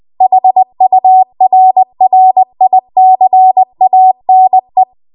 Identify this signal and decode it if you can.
Morse code, heard as a keyed tone, about 20 wpm, 750 Hz. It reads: HURRICANE
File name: Archive / 20 WPM / 20WPM